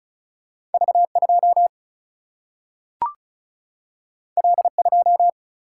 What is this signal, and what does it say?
Morse code, heard as a keyed tone, about 36 wpm, 700 Hz. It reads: V2 L2
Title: Single-Letter Single-Number 35wpm